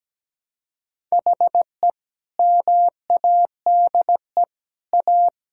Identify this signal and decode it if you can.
Morse code, heard as a keyed tone, about 17 wpm, 700 Hz. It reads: HE MADE A
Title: Sentences from Top 1000 Words - Review Part 6 - 17wpm